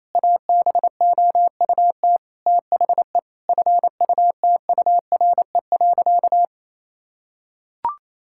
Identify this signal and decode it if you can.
Morse code, heard as a keyed tone, about 28 wpm, 700 Hz. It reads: ABOUT THE FUTURE.
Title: Sentences from Top 1000 Words - Review Part 5 - 28wpm